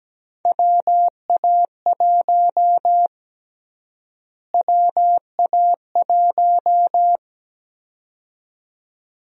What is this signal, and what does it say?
Morse code, heard as a keyed tone, about 17 wpm, 700 Hz. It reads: WA1 WA1
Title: USA Call Sign Prefixes - Repeated 2x - 17wpm